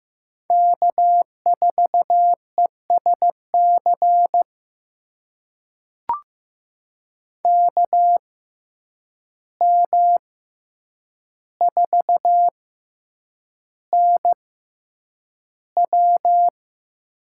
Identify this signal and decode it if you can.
Morse code, heard as a keyed tone, about 15 wpm, spaced as for 8 wpm, 700 Hz. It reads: K4ESC KM4NW
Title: USA Call Signs - 5 Characters - 6x Character Spacing - 15wpm